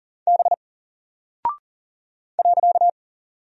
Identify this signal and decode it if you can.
Morse code, heard as a keyed tone, about 40 wpm, 700 Hz. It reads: B .
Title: Single Character (Letter:Number:Punctuation:Prosign) 40wpm